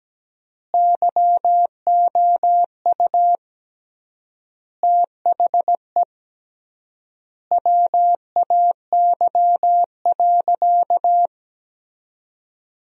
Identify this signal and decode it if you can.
Morse code, heard as a keyed tone, about 17 wpm, 700 Hz. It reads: YOU THE WAY.
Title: 3x Word-Spacing - Sentences from Top 300 Words - 17wpm